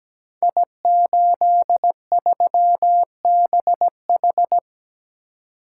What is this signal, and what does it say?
Morse code, heard as a keyed tone, about 17 wpm, 700 Hz. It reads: I83BH
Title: Code Groups - 5 Letter:Numbers - 17wpm